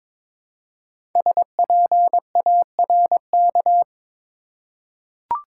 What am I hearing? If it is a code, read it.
Morse code, heard as a keyed tone, about 22 wpm, 700 Hz. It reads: SPARK
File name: 5 Letter Words - First 3 Letters Given - 22wpm